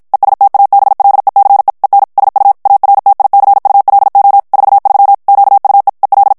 Morse code, heard as a keyed tone, about 55 wpm, 800 Hz. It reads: ESTABLECE EN SU ARTICULO 43 QUE EL
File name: ESPAGNOL TEXTE EN CLAIR 2 VITESSE 50 MOTS MINUTES